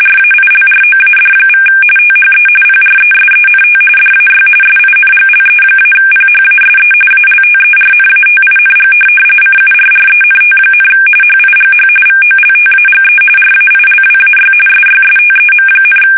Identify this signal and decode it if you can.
RTTY, about 45.45 baud. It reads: RAILROAD= THE PRESENT DAY SINGLE-PHASE ALTERNATING CURRENT, HIGH VOLTAGE RAILWAY SYSTEM,
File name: fsk850 / benjamin-g-lamme